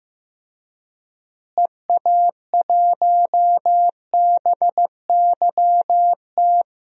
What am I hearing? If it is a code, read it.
Morse code, heard as a keyed tone, about 15 wpm, 700 Hz. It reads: EA1BYT
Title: DXCC-100 Call Signs - 6 Characters - Repeated 3x - 15wpm